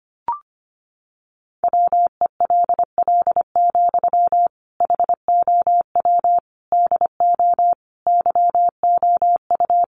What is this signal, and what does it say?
Morse code, heard as a keyed tone, about 25 wpm, 700 Hz. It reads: WELL, HOW DO YOU
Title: Sentences from Top 1100 Words 25wpm